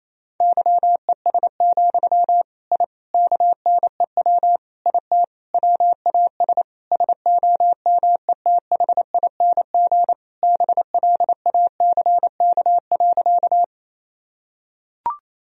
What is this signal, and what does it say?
Morse code, heard as a keyed tone, about 28 wpm, 700 Hz. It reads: YES, I KNEW IT WAS SOMETHING BLACK.
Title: Sentences from Top 1000 Words - Review Part 2 - 28wpm